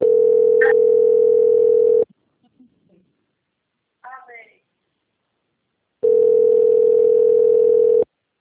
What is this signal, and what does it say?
Ringback tone (North America), 440 + 480 Hz, with speech.
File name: out- - - - - .50296